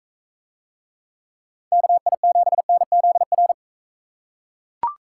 Morse code, heard as a keyed tone, about 43 wpm, 700 Hz. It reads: KI7NZR